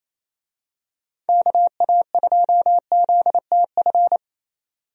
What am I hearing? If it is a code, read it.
Morse code, heard as a keyed tone, about 28 wpm, 700 Hz. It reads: KA2ZTF
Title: USA Call Signs - 6 Characters - 28wpm